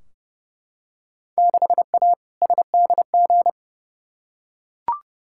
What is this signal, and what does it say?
Morse code, heard as a keyed tone, about 30 wpm, 700 Hz. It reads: 6A SDG
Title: Field Day Exchanges Encore - 30wpm